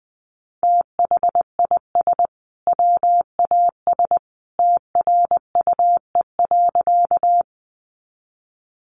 Morse code, heard as a keyed tone, about 20 wpm, 700 Hz. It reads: THIS WAS TRUE.
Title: Sentences from Top 400 Words 20wpm